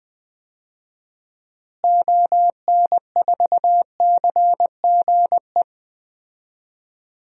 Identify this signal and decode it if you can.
Morse code, heard as a keyed tone, about 20 wpm, 700 Hz. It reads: ON4CGE
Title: DXCC-100 Call Signs - 6 Characters - Repeated 2x - 20wpm